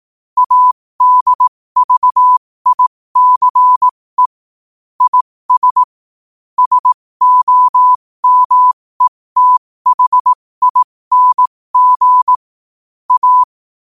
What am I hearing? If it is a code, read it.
Morse code, heard as a keyed tone, about 18 wpm, spaced as for 15 wpm, 1000 Hz. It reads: ADVICE IS SOMETHING A
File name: qotdfiles / 15wpm